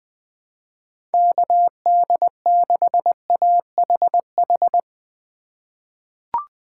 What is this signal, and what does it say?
Morse code, heard as a keyed tone, about 20 wpm, 700 Hz. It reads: KD6AHH
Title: USA Call Signs - 6 Characters - 4x Character Spacing - 20wpm